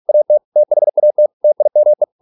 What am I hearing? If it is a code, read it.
Morse code, heard as a keyed tone, about 46 wpm, 600 Hz. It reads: AT THAT TIME